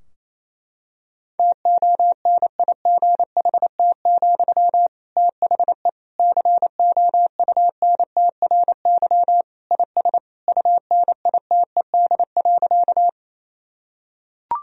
Morse code, heard as a keyed tone, about 28 wpm, 700 Hz. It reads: TONIGHT, THE COUNTRY IS UNITED.
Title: Sentences from Top 1000 Words - Review Part 4 - 28wpm